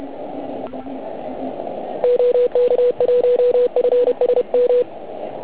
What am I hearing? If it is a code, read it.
Morse code, heard as a keyed tone, about 31 wpm, 490 Hz. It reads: OK1FSM
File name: Archiv / CUC129 / CUC129 pileup